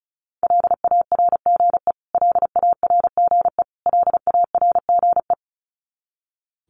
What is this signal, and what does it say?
Morse code, heard as a keyed tone, about 34 wpm, 700 Hz. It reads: LARGE LARGE LARGE
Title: Top 500 Words Repeated 35wpm